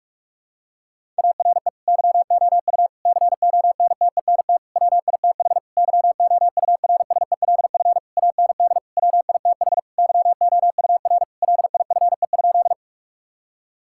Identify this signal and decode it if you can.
Morse code, heard as a keyed tone, about 41 wpm, 700 Hz. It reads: ARE YOU CONTENT WITH YOURSELF AND WITH YOUR LIFE?
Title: Sentences from Top 1000 Words - Review Part 6 - 45wpm